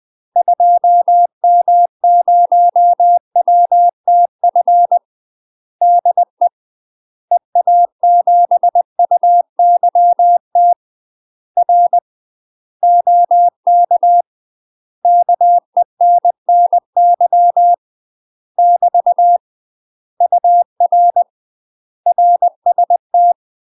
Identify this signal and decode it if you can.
Morse code, heard as a keyed tone, about 20 wpm, 700 Hz. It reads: 2M0WTF DE EA7UYT R OK KENNY <BT> UR RST